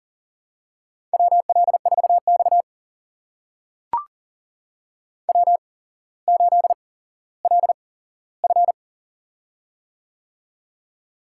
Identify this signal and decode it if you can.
Morse code, heard as a keyed tone, about 42 wpm, spaced as for 17 wpm, 700 Hz. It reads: WL4X W8LF
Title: USA Call Signs - 4 Characters - 8x Character Spacing - 40wpm